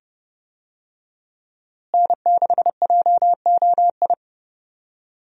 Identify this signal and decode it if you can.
Morse code, heard as a keyed tone, about 32 wpm, 700 Hz. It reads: N6JOI
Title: USA Call Signs - 5 Characters - Repeated 2x - 30wpm